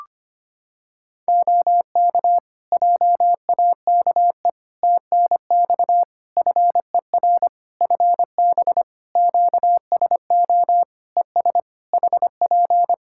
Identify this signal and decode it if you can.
Morse code, heard as a keyed tone, about 25 wpm, 700 Hz. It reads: OK JAKE TNX FER FB QSO ES HP
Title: Practice QSO - Exchange 4 - Ending - 25wpm